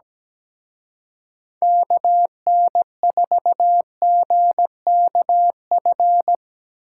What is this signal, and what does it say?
Morse code, heard as a keyed tone, about 18 wpm, 700 Hz. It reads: KN4GKF
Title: Speed Racing - USA 2x3 Call Signs 17wpm